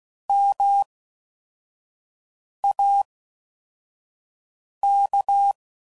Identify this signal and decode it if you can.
Morse code, heard as a keyed tone, about 16 wpm, spaced as for 4 wpm, 800 Hz. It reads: MAK